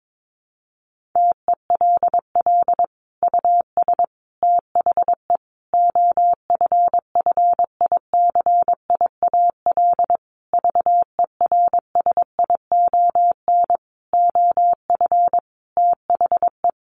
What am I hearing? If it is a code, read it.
Morse code, heard as a keyed tone, about 22 wpm, 700 Hz. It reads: TELL US THE OFFICIAL VERSION OF THE
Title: Sentences from Top 1400 Words 22wpm